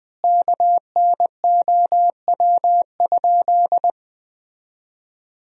Morse code, heard as a keyed tone, about 21 wpm, 700 Hz. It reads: KNOW?
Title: 2x Word-Spacing - Sentences from Top 100 Words - 20wpm